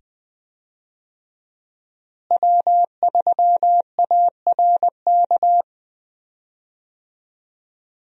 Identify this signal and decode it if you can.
Morse code, heard as a keyed tone, about 20 wpm, 700 Hz. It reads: W3ARK
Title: USA Call Signs - 5 Characters - Repeated 2x - 20wpm